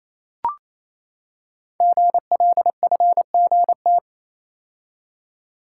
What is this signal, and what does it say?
Morse code, heard as a keyed tone, about 27 wpm, 700 Hz. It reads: GLFGT
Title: Code Groups - 5 Letters - 28wpm